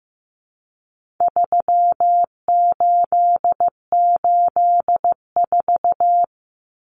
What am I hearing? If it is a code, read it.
Morse code, heard as a keyed tone, about 15 wpm, 700 Hz. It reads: 3884
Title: Common Cut Numbers - 4-Digit 15wpm